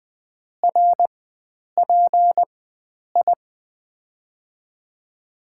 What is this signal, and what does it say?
Morse code, heard as a keyed tone, about 21 wpm, spaced as for 9 wpm, 700 Hz. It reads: RPI
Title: USA Call Signs - 4x Character Spacing - 20wpm